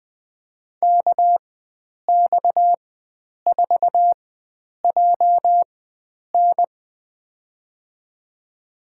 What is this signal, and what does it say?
Morse code, heard as a keyed tone, about 21 wpm, spaced as for 9 wpm, 700 Hz. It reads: KX4JN
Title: USA Call Signs - 4x Character Spacing - 20wpm